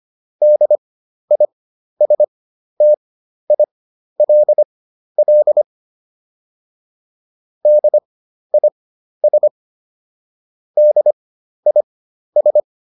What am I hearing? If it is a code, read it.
Morse code, heard as a keyed tone, about 25 wpm, spaced as for 12 wpm, 600 Hz. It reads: DISTILL DIS DIS